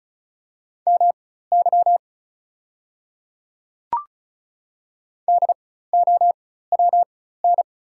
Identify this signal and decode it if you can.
Morse code, heard as a keyed tone, about 36 wpm, spaced as for 16 wpm, 690 Hz. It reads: MY DOWN
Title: 4x Character Spacing - Top 200 Words - 35wpm